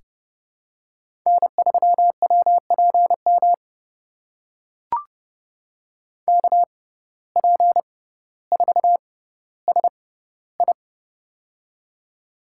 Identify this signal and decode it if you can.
Morse code, heard as a keyed tone, about 30 wpm, spaced as for 15 wpm, 700 Hz. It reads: N3WPM KP4SI
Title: USA Call Signs - 5 Characters - 6x Character Spacing - 30wpm